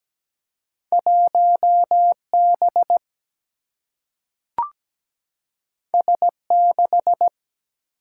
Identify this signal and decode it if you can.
Morse code, heard as a keyed tone, about 17 wpm, 700 Hz. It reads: 1B S6
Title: Single-Letter Single-Number 17wpm